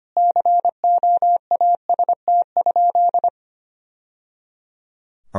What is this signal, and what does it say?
Morse code, heard as a keyed tone, about 25 wpm, 700 Hz. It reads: COAST?
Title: Sentences from Top 1000 Words - Review Part 4 - 25wpm